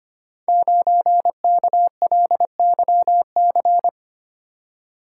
Morse code, heard as a keyed tone, about 25 wpm, 700 Hz. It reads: 9KLYC
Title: Code Groups - 5 Letter:Numbers - 25wpm